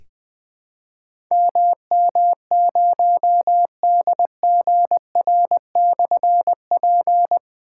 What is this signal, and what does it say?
Morse code, heard as a keyed tone, about 20 wpm, 700 Hz. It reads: MM0DGR/P
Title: Difficult Call Signs 20wpm